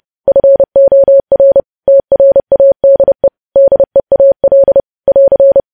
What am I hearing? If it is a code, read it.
Morse code, heard as a keyed tone, about 30 wpm, 550 Hz. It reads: FOR TRADE DEAL <AR>
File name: podcast / News headlines-1x-30x30